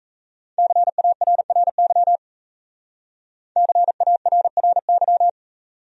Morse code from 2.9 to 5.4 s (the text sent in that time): CARRY